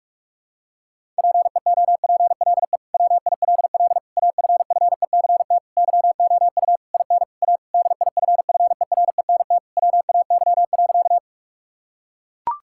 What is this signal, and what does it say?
Morse code, heard as a keyed tone, about 41 wpm, 700 Hz. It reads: PEOPLE WILL AFFECT YOU IN A DIFFERENT WAY.
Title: Sentences from Top 1000 Words - Review Part 6 - 45wpm